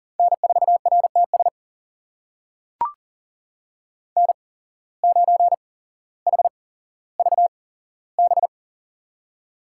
Morse code, heard as a keyed tone, about 38 wpm, spaced as for 17 wpm, 700 Hz. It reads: N4RTS N9HVB
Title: USA Call Signs - 5 Characters - 8x Character Spacing - 40wpm